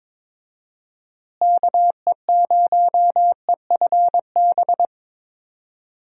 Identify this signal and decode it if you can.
Morse code, heard as a keyed tone, about 21 wpm, 700 Hz. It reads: KE0EFB
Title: USA Call Signs - 6 Characters - Repeated 3x - 22wpm